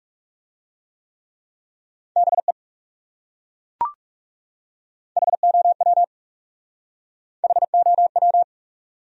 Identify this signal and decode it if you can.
Morse code, heard as a keyed tone, about 40 wpm, 700 Hz. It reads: BE HOW HOW